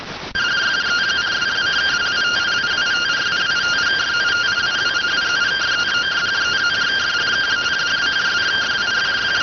Two-tone teleprinter signal, about 170 Hz shift.